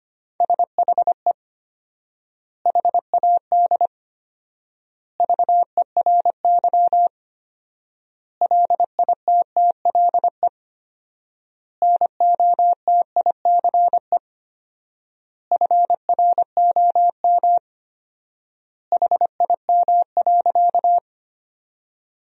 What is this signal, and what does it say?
Morse code, heard as a keyed tone, about 25 wpm, 700 Hz. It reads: SHE HAD VERY LITTLE NOTICE FROM HIM.
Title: 4x Word-Spacing - Sentences from Top 400 Words - 25wpm